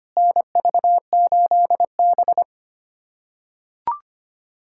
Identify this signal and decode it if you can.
Morse code, heard as a keyed tone, about 25 wpm, 700 Hz. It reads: NV8B
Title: USA Callsign DE Callsign - 4 Characters - 25wpm